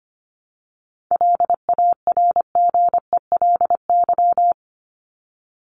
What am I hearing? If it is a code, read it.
Morse code, heard as a keyed tone, about 25 wpm, 700 Hz. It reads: LARGELY
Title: LY Suffix - Top 100 Words with Base Words 25wpm